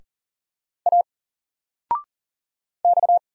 Morse code, heard as a keyed tone, about 40 wpm, 700 Hz. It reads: A X
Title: Single Character (Letter:Number:Punctuation:Prosign) 40wpm